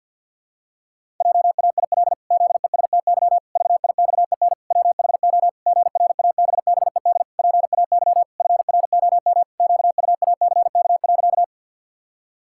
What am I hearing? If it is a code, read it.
Morse code, heard as a keyed tone, about 51 wpm, 700 Hz. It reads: JAIL ZESTY VIXEN WHO GRABBED PAY FROM QUACK.
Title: Pangram Challenge v2 - 50wpm